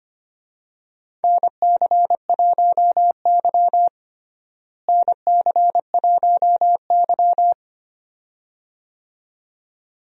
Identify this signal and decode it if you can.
Morse code, heard as a keyed tone, about 25 wpm, 700 Hz. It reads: NC1Y NC1Y
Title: USA Call Signs - 4 Characters - Repeated 2x - 25wpm